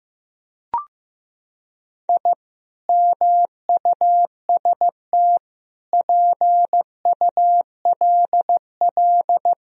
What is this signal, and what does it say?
Morse code, heard as a keyed tone, about 15 wpm, 700 Hz. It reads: I MUST PULL
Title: Sentences from Top 1000 Words - Review Part 2 - 15wpm